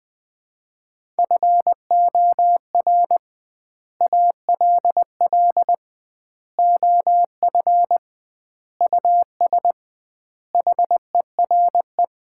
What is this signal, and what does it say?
Morse code, heard as a keyed tone, about 20 wpm, 700 Hz. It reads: FOR ALL OF US HERE
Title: 2x Word-Spacing - Sets of 5 Words - 20wpm